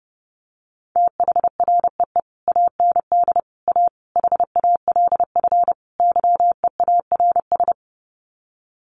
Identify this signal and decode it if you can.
Morse code, heard as a keyed tone, about 30 wpm, 700 Hz. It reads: THREE AND A HALF YEARS
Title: Sets of 5 Words Encore2 30wpm